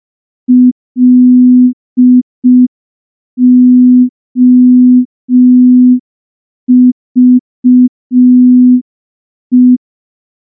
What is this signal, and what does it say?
Morse code, heard as a keyed tone, about 5 wpm, 250 Hz. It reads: LOVE
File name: morse / morse-training-05WPM